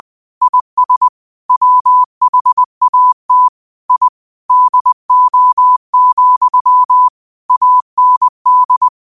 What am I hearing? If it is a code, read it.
Morse code, heard as a keyed tone, about 20 wpm, 1000 Hz. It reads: IS WHAT I DO, AND
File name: qotdfiles / 20wpm